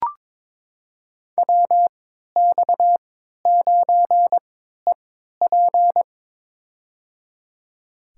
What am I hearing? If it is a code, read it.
Morse code, heard as a keyed tone, about 22 wpm, spaced as for 12 wpm, 700 Hz. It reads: WX9EP
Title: USA Call Signs - 5 Characters - 3x Character Spacing - 22wpm